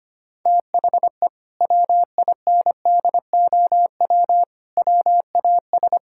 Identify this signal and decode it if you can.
Morse code, heard as a keyed tone, about 25 wpm, 700 Hz. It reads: THE WINDOW WAS
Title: Sentences from Top 1600 Words 25wpm